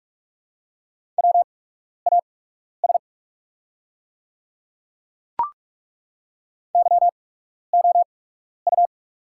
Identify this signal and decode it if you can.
Morse code, heard as a keyed tone, about 42 wpm, spaced as for 12 wpm, 700 Hz. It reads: WAS YOU
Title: 8x Character Spacing - Top 100 Words - 45wpm